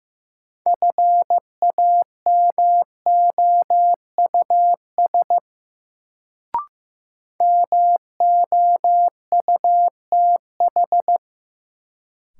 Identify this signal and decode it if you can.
Morse code, heard as a keyed tone, about 16 wpm, 700 Hz. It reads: FAMOUS MOUTH
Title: Sentences from Top 1500 Words - New Word Drill - 15wpm